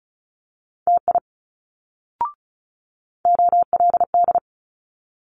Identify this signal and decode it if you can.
Morse code, heard as a keyed tone, about 36 wpm, 700 Hz. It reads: TI OLD
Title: Multi-letter Phonemes Quick Fire 35wpm